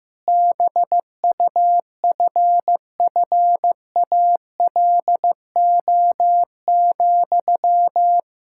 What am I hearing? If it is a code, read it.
Morse code, heard as a keyed tone, about 15 wpm, 700 Hz. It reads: BUFFALO,
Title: Practice QSO - Exchange 1 - Beginning - 15wpm